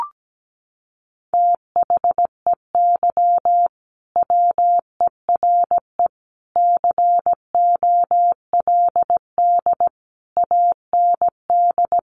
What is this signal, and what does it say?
Morse code, heard as a keyed tone, about 17 wpm, 700 Hz. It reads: THEY WERE COLD AND
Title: Sentences from Top 1500 Words 17wpm